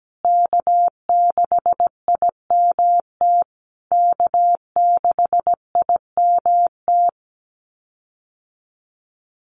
Morse code, heard as a keyed tone, about 17 wpm, 700 Hz. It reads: K6IMT K6IMT
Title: USA Call Signs Repeated 3x 17wpm